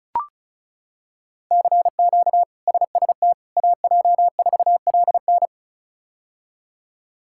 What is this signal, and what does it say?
Morse code, heard as a keyed tone, about 34 wpm, 700 Hz. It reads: CQ SST AJ4LN
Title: K1USN SST QSO Practice - 35wpm